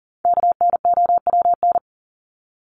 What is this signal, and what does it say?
Morse code, heard as a keyed tone, about 40 wpm, 700 Hz. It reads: KNOWN
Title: Top 501-750 Words 40wpm